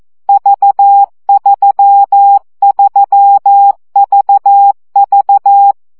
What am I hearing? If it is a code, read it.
Morse code, heard as a keyed tone, about 14 wpm, 800 Hz. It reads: V33VV
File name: CW gyakorlas v 3 12 baud